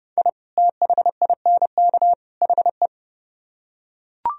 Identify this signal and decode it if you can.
Morse code, heard as a keyed tone, about 30 wpm, 700 Hz. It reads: I THINK HE